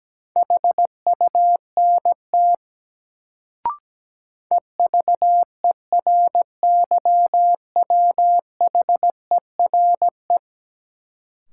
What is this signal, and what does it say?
Morse code, heard as a keyed tone, about 17 wpm, 700 Hz. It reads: HUNT EVERYWHERE
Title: Sentences from Top 1600 Words - New Word Drill Alt - 17wpm